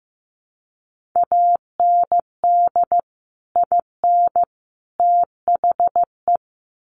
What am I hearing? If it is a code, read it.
Morse code, heard as a keyed tone, about 15 wpm, 700 Hz. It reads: AND IN THE
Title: Sets of 3 Words 15wpm